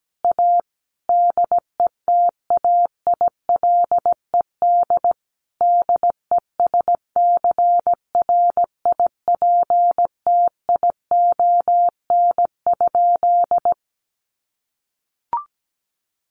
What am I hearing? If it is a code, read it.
Morse code, heard as a keyed tone, about 17 wpm, 700 Hz. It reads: A DETAILED DESCRIPTION?
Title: Sentences from Top 1500 Words 17wpm